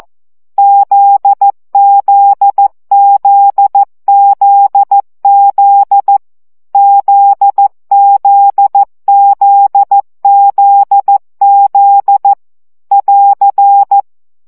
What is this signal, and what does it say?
Morse code, heard as a keyed tone, about 14 wpm, 800 Hz. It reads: ZZZZZ ZZZZZ <AR>
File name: CW gyakorlas z 12 baud